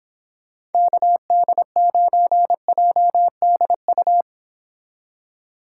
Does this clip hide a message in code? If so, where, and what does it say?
Yes: Morse code from 0.7 to 4.2 s, heard as a keyed tone, about 30 wpm, 700 Hz. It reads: KD9JDU